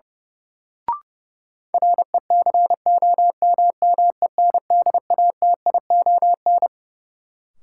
Morse code, heard as a keyed tone, about 30 wpm, 700 Hz. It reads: RECOMMENDATION
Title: Sentences from Top 1400 Words - New Word Drill - 30wpm